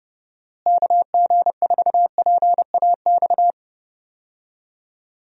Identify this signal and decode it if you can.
Morse code, heard as a keyed tone, about 32 wpm, 700 Hz. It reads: KG4PAX
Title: USA Call Signs - 6 Characters - Repeated 2x - 30wpm